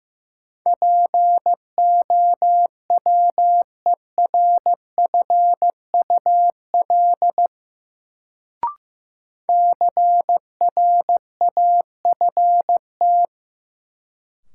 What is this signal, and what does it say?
Morse code, heard as a keyed tone, about 15 wpm, 700 Hz. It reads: POWERFUL CRAFT
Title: Sentences from Top 1000 Words - Review Part 6 - New Word Drill - 15wpm